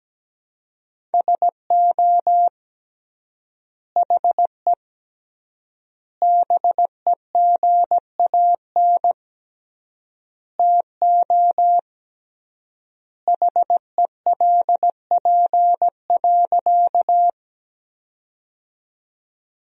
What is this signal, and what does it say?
Morse code, heard as a keyed tone, about 17 wpm, 700 Hz. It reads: SO HE BEGAN TO HELP.
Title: 3x Word-Spacing - Sentences from Top 300 Words - 17wpm